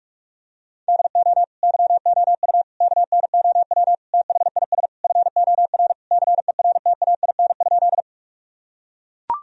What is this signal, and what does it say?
Morse code, heard as a keyed tone, about 41 wpm, 700 Hz. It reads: DO YOU KNOW THIS FOR CERTAIN?
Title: Sentences from Top 1000 Words - Review Part 2 - 45wpm